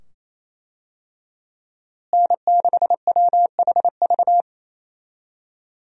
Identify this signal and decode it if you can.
Morse code, heard as a keyed tone, about 28 wpm, 700 Hz. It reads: N6WHV
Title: USA Call Signs - 5 Characters - Repeated 2x - 28wpm